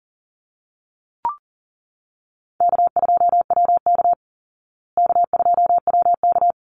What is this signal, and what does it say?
Morse code, heard as a keyed tone, about 40 wpm, 700 Hz. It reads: K2WK K2WK
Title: Easy Call Signs Repeated 2x - 40wpm